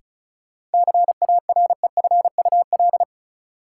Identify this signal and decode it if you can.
Morse code, heard as a keyed tone, about 34 wpm, 700 Hz. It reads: CAREFUL
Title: Sentences from Top 1600 Words - New Word Drill - 35wpm